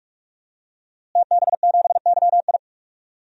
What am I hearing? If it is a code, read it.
Morse code, heard as a keyed tone, about 41 wpm, 700 Hz. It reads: TB7YI
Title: DXCC-100 Call Signs - 5 Characters - Repeated 2x - 45wpm